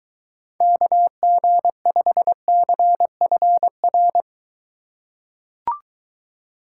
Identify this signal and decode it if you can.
Morse code, heard as a keyed tone, about 23 wpm, 700 Hz. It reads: KG5CFR